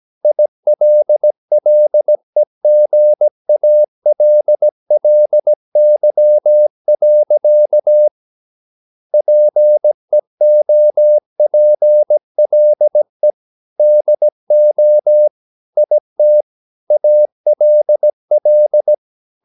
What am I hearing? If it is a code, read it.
Morse code, heard as a keyed tone, about 17 wpm, 600 Hz. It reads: ILLEGALLY. PEOPLE DO IT ALL
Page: Romanen Never af Ken Follet på engelsk.